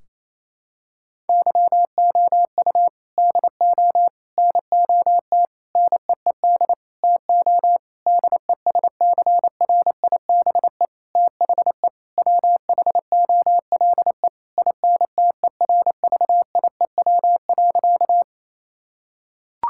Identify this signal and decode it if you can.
Morse code, heard as a keyed tone, about 28 wpm, 700 Hz. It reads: YOU DO NOT NEED TO DESCRIBE THE WHOLE INTERVIEW.
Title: Sentences from Top 1600 Words 28wpm